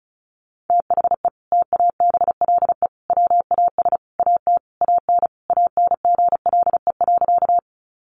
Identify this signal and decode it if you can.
Morse code, heard as a keyed tone, about 34 wpm, 700 Hz. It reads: THE TABLE WAS AT AN ANGLE.
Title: Sentences from Top 1500 Words 35wpm